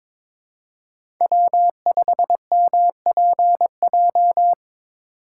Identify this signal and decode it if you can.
Morse code, heard as a keyed tone, about 26 wpm, 700 Hz. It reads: W5MPJ
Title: Speed Racing - USA 1x3 Call Signs 22wpm